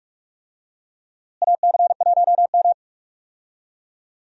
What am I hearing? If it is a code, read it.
Morse code, heard as a keyed tone, about 46 wpm, 700 Hz. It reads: AC1M
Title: USA Call Signs - 4 Characters - Repeated 3x - 45wpm